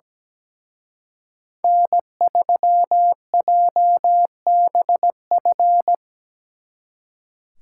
Morse code, heard as a keyed tone, about 20 wpm, 700 Hz. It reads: N3JBF
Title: Speed Racing - USA 1x3 Call Signs 17wpm